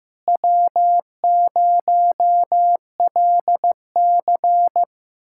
Morse code, heard as a keyed tone, about 15 wpm, 700 Hz. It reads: W0LC
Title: USA Callsign DE Callsign - 4 Characters - 15wpm